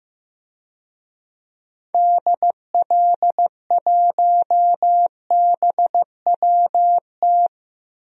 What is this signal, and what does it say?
Morse code, heard as a keyed tone, about 15 wpm, 700 Hz. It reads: DL1BWT
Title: DXCC-100 Call Signs - 6 Characters - 15wpm